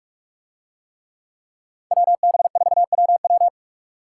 Morse code, heard as a keyed tone, about 44 wpm, 700 Hz. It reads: WB4WW